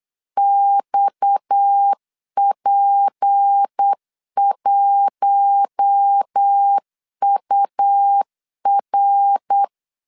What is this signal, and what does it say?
Morse code, heard as a keyed tone, about 8 wpm, 800 Hz. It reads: XP1UR
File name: 8wpm